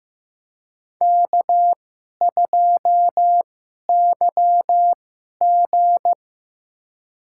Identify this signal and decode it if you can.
Morse code, heard as a keyed tone, about 15 wpm, spaced as for 11 wpm, 700 Hz. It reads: K2YG
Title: USA Call Signs - 4 Characters - 2x Character Spacing - 15wpm